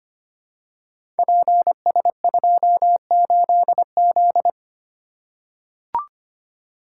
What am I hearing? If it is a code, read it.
Morse code, heard as a keyed tone, about 25 wpm, 700 Hz. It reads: PS28Z
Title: Code Groups - 5 Letter:Numbers - 25wpm